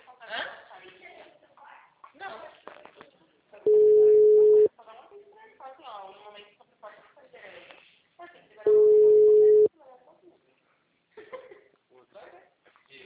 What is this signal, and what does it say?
Ringback tone (Europe, 425 Hz), with speech.
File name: out- -2009- - - .30